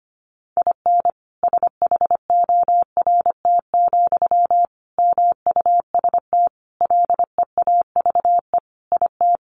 Morse code, heard as a keyed tone, about 25 wpm, 700 Hz. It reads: IN SHORT, MUST LEAVE IT
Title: Sentences from Top 1000 Words 25wpm